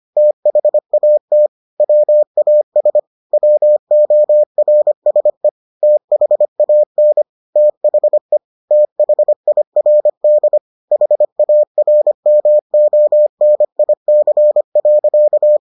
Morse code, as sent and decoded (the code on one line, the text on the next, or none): - .... .- - / .-- .- ... / .-- --- .-. ... / - .... .- -. / - .... / - .... .. .-. -.. / .... .- .-. -- --- -. .. -.-. .-.-.-
THAT WAS WORSE THAN THE THIRD HARMONIC.